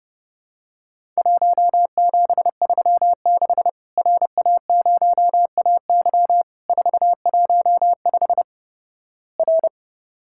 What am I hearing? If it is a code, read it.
Morse code, heard as a keyed tone, about 30 wpm, 700 Hz. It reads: 1736 RA0AY 415 R
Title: Worked All Europe DX Contest - QTC Exchange - 30wpm